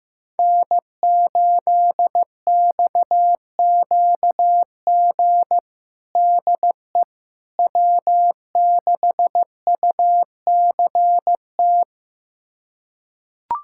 Morse code, heard as a keyed tone, about 15 wpm, 700 Hz. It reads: N8XQG DE W6UCT
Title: USA Callsign DE Callsign - 5 Characters - 15wpm